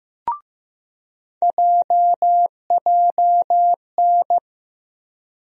Morse code, heard as a keyed tone, about 15 wpm, 700 Hz. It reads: JJN
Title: USA Call Sign Postfixes - 3 Letters 15wpm